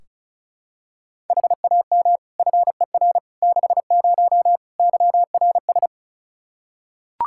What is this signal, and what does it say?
Morse code, heard as a keyed tone, about 36 wpm, 700 Hz. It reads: HAM FER 60 YRS
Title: Practice QSO - Exchange 3 - More Info - 35wpm